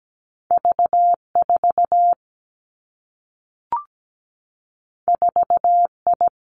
Morse code, heard as a keyed tone, about 17 wpm, 700 Hz. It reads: V4 4I
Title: Two Challenging Letter-Number Combinations 17wpm